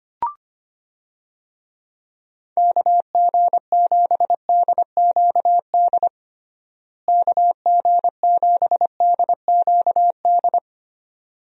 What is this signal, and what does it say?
Morse code, heard as a keyed tone, about 25 wpm, 700 Hz. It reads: KG7DQD KG7DQD
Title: USA Call Signs - 6 Characters - Repeated 2x - 25wpm